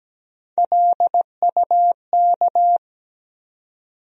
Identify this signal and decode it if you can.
Morse code, heard as a keyed tone, about 17 wpm, 700 Hz. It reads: LUK
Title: USA Call Sign Postfixes - 3 Letters 17wpm